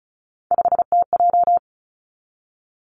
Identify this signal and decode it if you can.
Morse code, heard as a keyed tone, about 36 wpm, 700 Hz. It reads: 5TJ
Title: Three Challenging Letter-Number Combinations 35wpm